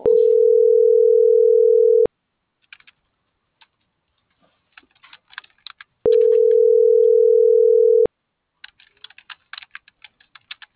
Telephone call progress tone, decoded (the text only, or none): ringback tone